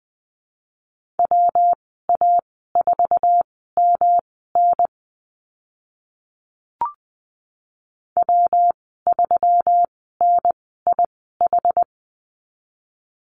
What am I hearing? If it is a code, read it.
Morse code, heard as a keyed tone, about 20 wpm, spaced as for 14 wpm, 700 Hz. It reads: WA4MN W3NIH
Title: 122 Lesson 20g - Number 3 - Call Signs 20wpm